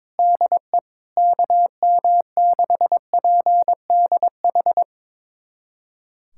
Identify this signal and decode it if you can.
Morse code, heard as a keyed tone, about 22 wpm, 700 Hz. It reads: DE KM6PDH
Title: USA Callsign DE Callsign - 6 Characters - 22wpm